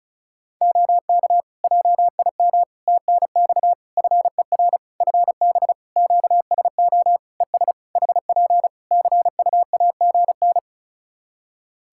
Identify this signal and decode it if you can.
Morse code, heard as a keyed tone, about 36 wpm, 700 Hz. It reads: OK JIM TNX FER FB QSO ES HP CUAGN
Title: Practice QSO - Exchange 4 - Ending - 35wpm